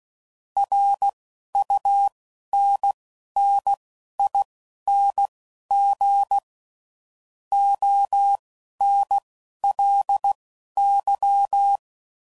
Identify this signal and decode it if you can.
Morse code, heard as a keyed tone, about 16 wpm, spaced as for 11 wpm, 800 Hz. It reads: RUNNING ONLY
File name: CW Code Groups @10WPM / QSOs / QSO kc7sac